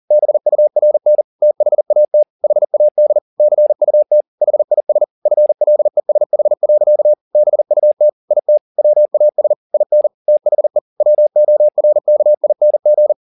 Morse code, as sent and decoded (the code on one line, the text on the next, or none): -... ..- .-. -. / - .... .- - / .... .- -.. / -.-. ..- - / .... .. ... / ..-. .-.. ... .... .-.-.- / -... ..- - / .. - / .-- .- ... / .. -. / - .... / .-- --- .-. -.- .. -. --.
BURN THAT HAD CUT HIS FLESH. BUT IT WAS IN THE WORKING